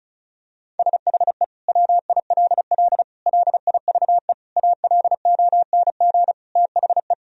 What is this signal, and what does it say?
Morse code, heard as a keyed tone, about 36 wpm, 700 Hz. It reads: SHE WILL LIVE ALONG THE
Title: Sentences from Top 1000 Words - Review Part 4 - 35wpm